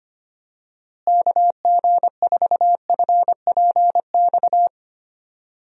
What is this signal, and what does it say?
Morse code, heard as a keyed tone, about 25 wpm, 700 Hz. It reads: KG4FPX
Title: USA Call Signs - 6 Characters - Repeated 3x - 25wpm